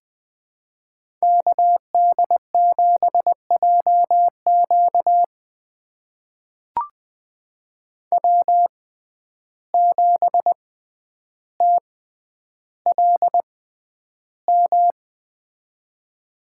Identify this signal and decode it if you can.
Morse code, heard as a keyed tone, about 20 wpm, spaced as for 10 wpm, 700 Hz. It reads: KD7JQ W7TLM
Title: USA Call Signs - 5 Characters - 6x Character Spacing - 20wpm